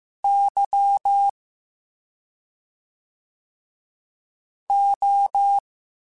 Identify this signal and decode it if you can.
Morse code, heard as a keyed tone, about 15 wpm, spaced as for 2 wpm, 800 Hz. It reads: YO